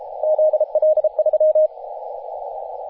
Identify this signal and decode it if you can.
Morse code, heard as a keyed tone, about 33 wpm, 620 Hz, with radio noise. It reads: ZL3
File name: t30cow-18-072mhz-cw